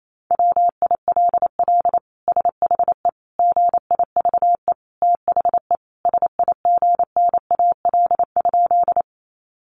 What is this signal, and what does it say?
Morse code, heard as a keyed tone, about 28 wpm, 700 Hz. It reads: WILL SHE GIVE THE SIGNAL?
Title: Sentences from Top 1400 Words 28wpm